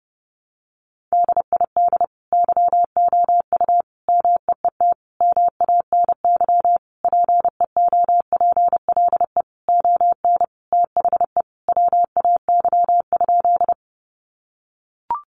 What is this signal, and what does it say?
Morse code, heard as a keyed tone, about 30 wpm, 700 Hz. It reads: DID YOU MEET MANY PEOPLE ON THE WAY?
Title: Sentences from Top 1100 Words 30wpm